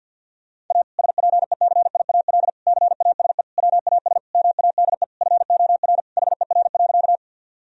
Morse code, heard as a keyed tone, about 50 wpm, 700 Hz. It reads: A SPECIAL CASE WAS MADE FOR HER.
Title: Sentences from Top 1000 Words - Review Part 3 - 50wpm